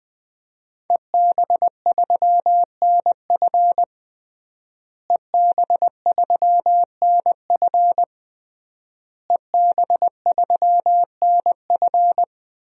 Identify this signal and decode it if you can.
Morse code, heard as a keyed tone, about 20 wpm, 700 Hz. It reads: EB3NF EB3NF EB3NF
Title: DXCC-100 Call Signs - 5 Characters - Repeated 3x - 20wpm